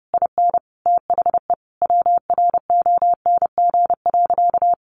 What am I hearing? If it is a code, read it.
Morse code, heard as a keyed tone, about 30 wpm, 700 Hz. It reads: IN THE WRONG.
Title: Sentences from Top 1200 Words 30wpm